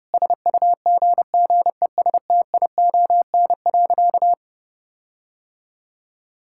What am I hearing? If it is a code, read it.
Morse code, heard as a keyed tone, about 32 wpm, 700 Hz. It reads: SUGGESTION.
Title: Sentences from Top 1600 Words 30wpm